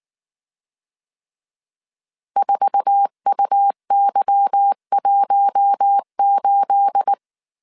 Morse code, heard as a keyed tone, about 19 wpm, 800 Hz. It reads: 4UY18
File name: CW / DET / 1 9 1500C S A 18wpm 2459